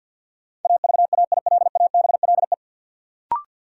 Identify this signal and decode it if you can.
Morse code, heard as a keyed tone, about 50 wpm, 700 Hz. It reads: AVAILABLE
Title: Sentences from Top 600 Words - New Word Drill - 50wpm